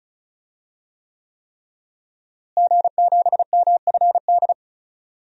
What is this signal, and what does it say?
Morse code, heard as a keyed tone, about 36 wpm, 700 Hz. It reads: G7MFD
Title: DXCC-100 Call Signs - 5 Characters - Repeated 3x - 35wpm